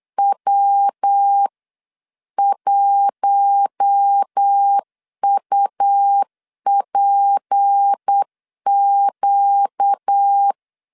Morse code, heard as a keyed tone, about 8 wpm, 800 Hz. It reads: W1UPQ
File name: 8wpm